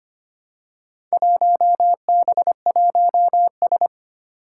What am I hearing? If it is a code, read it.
Morse code, heard as a keyed tone, about 25 wpm, 700 Hz. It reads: 1B1S
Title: Four Challenging Letter-Number Combinations 25wpm